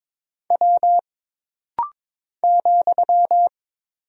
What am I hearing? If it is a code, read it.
Morse code, heard as a keyed tone, about 22 wpm, 700 Hz. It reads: W ,
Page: This rapid-fire timing is geared to push your ICR (Instant Character Recognition) skills.